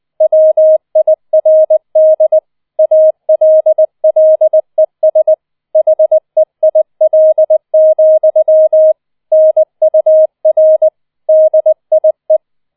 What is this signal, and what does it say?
Morse code, heard as a keyed tone, about 20 wpm, 630 Hz. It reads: WIRD ALLES HEIL, NUR DIE